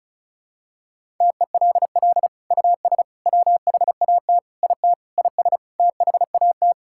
Morse code, heard as a keyed tone, about 36 wpm, 700 Hz. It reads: TELL US WHAT IT IS THAT
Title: Sentences from Top 1000 Words - Review Part 6 - 35wpm